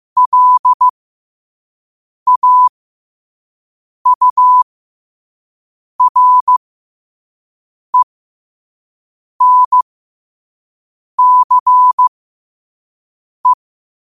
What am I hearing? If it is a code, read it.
Morse code, heard as a keyed tone, about 15 wpm, spaced as for 5 wpm, 1000 Hz. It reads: LAURENCE